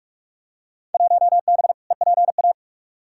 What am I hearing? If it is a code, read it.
Morse code, heard as a keyed tone, about 46 wpm, 700 Hz. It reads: 1B EPA
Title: Field Day Exchanges Encore - 45wpm